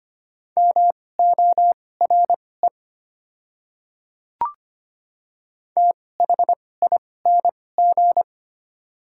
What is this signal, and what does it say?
Morse code, heard as a keyed tone, about 25 wpm, spaced as for 18 wpm, 700 Hz. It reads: MORE THING
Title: 2x Character Spacing - Top 200 Words - 25wpm